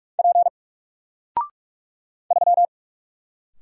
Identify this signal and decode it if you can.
Morse code, heard as a keyed tone, about 44 wpm, 700 Hz. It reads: P 3
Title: Single Character (Letter:Number:Punctuation:Prosign) Alt 45wpm